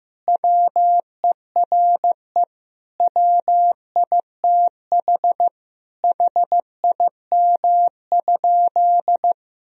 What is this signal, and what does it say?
Morse code, heard as a keyed tone, about 15 wpm, 700 Hz. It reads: WERE WITH HIM?
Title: Sentences from Top 1000 Words - Review Part 1 - 15wpm